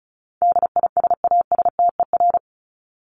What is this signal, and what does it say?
Morse code, heard as a keyed tone, about 34 wpm, 700 Hz. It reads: DISASTER
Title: Top 1501-1750 Words 35wpm